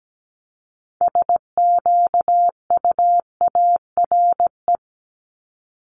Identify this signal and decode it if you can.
Morse code, heard as a keyed tone, about 17 wpm, 700 Hz. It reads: SQUARE
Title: Multi-letter Phonemes 17wpm